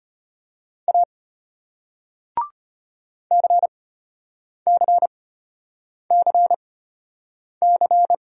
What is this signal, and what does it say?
Morse code, heard as a keyed tone, about 33 wpm, spaced as for 8 wpm, 700 Hz. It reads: A CCCC